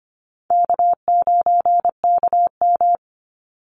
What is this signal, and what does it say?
Morse code, heard as a keyed tone, about 25 wpm, 700 Hz. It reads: K9KM
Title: Easy Call Signs Repeated 3x - 25wpm